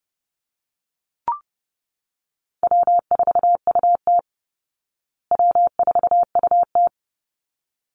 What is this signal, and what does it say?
Morse code, heard as a keyed tone, about 30 wpm, 700 Hz. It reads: W4UT W4UT
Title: Call Signs Repeated 3x - 30wpm